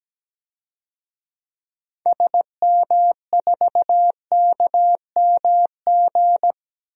Morse code, heard as a keyed tone, about 17 wpm, 700 Hz. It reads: SM4KMG
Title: DXCC-100 Call Signs - 6 Characters - Repeated 3x - 17wpm